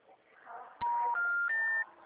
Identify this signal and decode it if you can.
Special information tone, with noise.